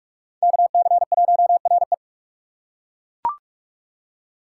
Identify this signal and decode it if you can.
Morse code, heard as a keyed tone, about 41 wpm, 700 Hz. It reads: KC1RE
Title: USA Call Signs - 5 Characters - 2x Character Spacing - 45wpm